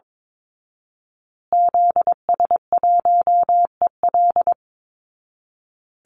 Morse code, heard as a keyed tone, about 22 wpm, 700 Hz. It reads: ZS1EL
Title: Call Signs Repeated 2x - 22wpm